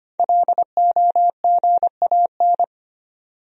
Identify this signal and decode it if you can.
Morse code, heard as a keyed tone, about 25 wpm, 700 Hz. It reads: LOGAN
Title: USA Common Names - 25wpm